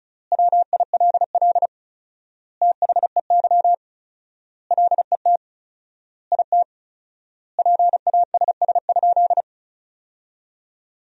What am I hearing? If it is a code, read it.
Morse code, heard as a keyed tone, about 36 wpm, 700 Hz. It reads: WILL THEY LET IT PASS?
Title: 4x Word-Spacing - Sentences from Top 400 Words - 35wpm